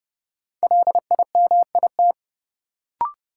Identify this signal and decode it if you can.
Morse code, heard as a keyed tone, about 30 wpm, 700 Hz. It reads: LIMIT
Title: Sentences from Top 1500 Words - New Word Drill - 30wpm